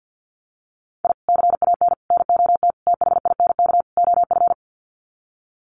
Morse code, heard as a keyed tone, about 50 wpm, 700 Hz. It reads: I CAN NOT THINK OF
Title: Sets of 5 Words 50wpm